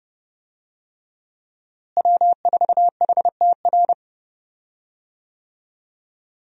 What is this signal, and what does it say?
Morse code, heard as a keyed tone, about 31 wpm, 700 Hz. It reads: W4HTR
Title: USA Call Signs - 5 Characters - Repeated 3x - 30wpm